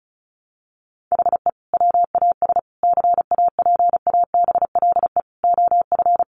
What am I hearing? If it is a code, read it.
Morse code, heard as a keyed tone, about 34 wpm, 700 Hz. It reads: HE WAS CAPABLE OF
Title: Sentences from Top 1200 Words 35wpm